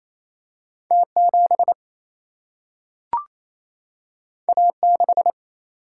Morse code, heard as a keyed tone, about 28 wpm, 700 Hz. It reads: T7 A6
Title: Single-Letter Single-Number 28wpm